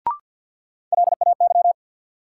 Morse code, heard as a keyed tone, about 53 wpm, 700 Hz. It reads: LAY
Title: Sentences from Top 400 Words - New Word Drill - 50wpm